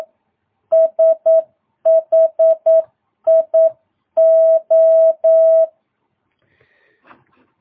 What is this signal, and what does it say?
Morse code, heard as a keyed tone, about 9 wpm, 650 Hz. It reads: SHIO
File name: jh8cbh / 2022cw